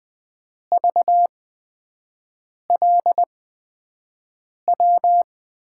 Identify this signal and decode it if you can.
Morse code, heard as a keyed tone, about 20 wpm, spaced as for 5 wpm, 700 Hz. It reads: VLW
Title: USA Call Signs - 6 Characters - 8x Character Spacing - 20wpm